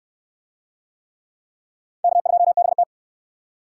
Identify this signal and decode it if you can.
Morse code, heard as a keyed tone, about 66 wpm, 700 Hz. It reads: K2BT